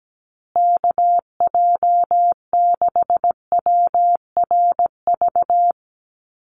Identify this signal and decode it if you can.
Morse code, heard as a keyed tone, about 17 wpm, 700 Hz. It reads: KJ6WRV
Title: USA Call Signs Repeated 2x 17wpm